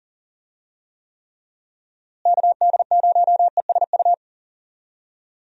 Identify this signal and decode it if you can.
Morse code, heard as a keyed tone, about 40 wpm, 700 Hz. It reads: KD0ESU
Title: USA Call Signs - 6 Characters - Repeated 3x - 40wpm